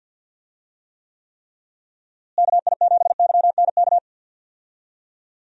Morse code, heard as a keyed tone, about 50 wpm, 700 Hz. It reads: KI7YNK
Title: USA Call Signs - 6 Characters - Repeated 3x - 50wpm